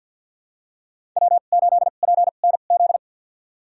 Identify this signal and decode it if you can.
Morse code, heard as a keyed tone, about 51 wpm, spaced as for 36 wpm, 700 Hz. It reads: W8PNZ
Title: USA Call Signs - 5 Characters - 2x Character Spacing - 50wpm